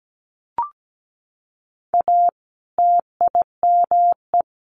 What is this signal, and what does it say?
Morse code, heard as a keyed tone, about 17 wpm, 700 Hz. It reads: A TIME
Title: A Sound Of Thunder 17wpm